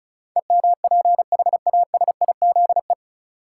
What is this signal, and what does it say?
Morse code, heard as a keyed tone, about 36 wpm, 700 Hz. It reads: EMPHASIZE
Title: Sentences from Top 1500 Words - New Word Drill Alt - 35wpm